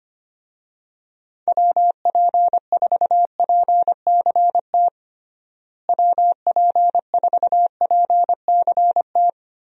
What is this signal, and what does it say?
Morse code, heard as a keyed tone, about 25 wpm, 700 Hz. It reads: WP4PCT WP4PCT
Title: USA Call Signs - 6 Characters - Repeated 3x - 25wpm